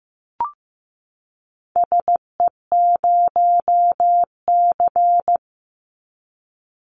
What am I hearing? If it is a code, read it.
Morse code, heard as a keyed tone, about 15 wpm, 700 Hz. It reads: SE0C
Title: Call Signs Repeated 3x - 15wpm